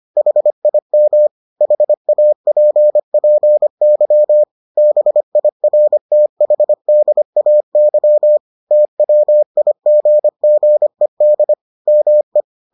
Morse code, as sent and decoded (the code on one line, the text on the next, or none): .... .. -- / .... .- .--. .--. -.-- / -... .. .-. - .... -.. .- -.-- / - .-- .. --. --. -.. / -- .
HIM HAPPY BIRTHDAY TWIGGED ME